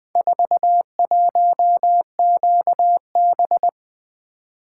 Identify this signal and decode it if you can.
Morse code, heard as a keyed tone, about 21 wpm, 700 Hz. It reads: 41QB
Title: Four Letter-Number Combinations - 20wpm